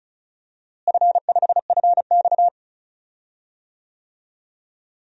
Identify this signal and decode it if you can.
Morse code, heard as a keyed tone, about 36 wpm, 700 Hz. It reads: F5FX
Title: DXCC-100 Call Signs - 4 Characters - Repeated 3x - 35wpm